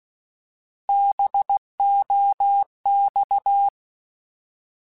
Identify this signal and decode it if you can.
Morse code, heard as a keyed tone, about 16 wpm, 800 Hz. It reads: BOX
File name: Alphabetical / Alphabetical mp3s / Bs @15